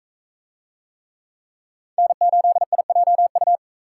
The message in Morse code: N8IJU

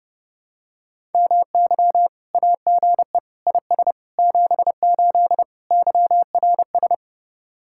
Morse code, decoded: MY AGE IS 78 YRS

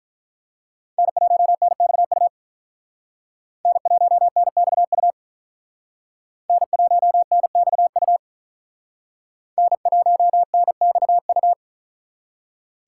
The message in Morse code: N1NXU N1NXU N1NXU N1NXU

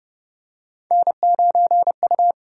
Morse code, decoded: N9U

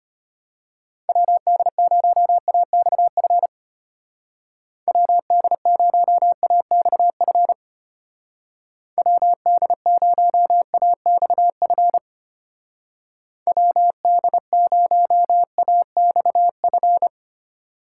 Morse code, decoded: WD0AXF WD0AXF WD0AXF WD0AXF

